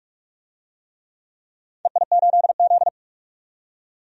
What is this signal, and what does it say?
Morse code, heard as a keyed tone, about 41 wpm, 700 Hz. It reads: EI8Z
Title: DXCC-100 Call Signs - 4 Characters - 45wpm